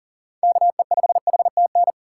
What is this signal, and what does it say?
Morse code, heard as a keyed tone, about 40 wpm, 700 Hz. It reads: KE5HTN
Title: USA Call Signs - 6x Character Spacing - 40wpm